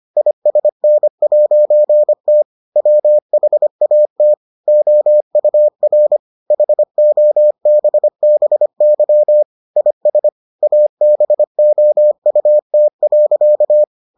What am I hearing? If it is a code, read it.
Morse code, heard as a keyed tone, about 25 wpm, 600 Hz. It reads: ISN'T WHAT OUR HOBBY IS ABOUT.